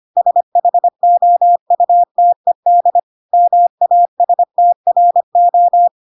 Morse code, heard as a keyed tone, about 25 wpm, 700 Hz. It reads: SHOUTED MASTRO
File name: Pinocchio / Pinocchio ch2 25 25 ik1hge